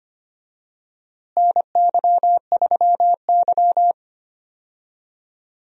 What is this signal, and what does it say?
Morse code, heard as a keyed tone, about 25 wpm, 700 Hz. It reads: NY3Y
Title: USA Call Signs - 4 Characters - Repeated 3x - 25wpm